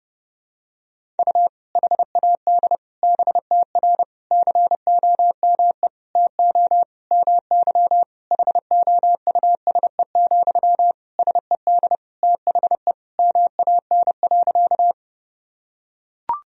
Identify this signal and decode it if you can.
Morse code, heard as a keyed tone, about 32 wpm, 700 Hz. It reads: U HAD BTR COME TO MY HOUSE, SED THE MAN.
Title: Sentences from Top 400 Words and Abbreviations 30wpm